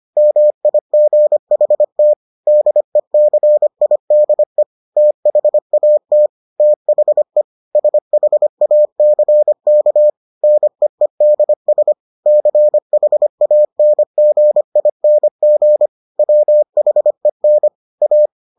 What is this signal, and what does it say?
Morse code, heard as a keyed tone, about 25 wpm, 600 Hz. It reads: MIGHT DECIDE THAT THE SHACK NEEDS CHANGING WHEN A